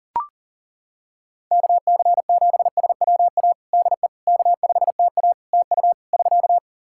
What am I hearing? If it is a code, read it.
Morse code, heard as a keyed tone, about 40 wpm, 700 Hz. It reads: KC7SWA DE K5TA TU <SK>
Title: Practice QSO - Exchange 4 - Ending - 40wpm